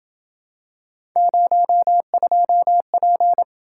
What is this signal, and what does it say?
Morse code, heard as a keyed tone, about 28 wpm, 700 Hz. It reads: 02P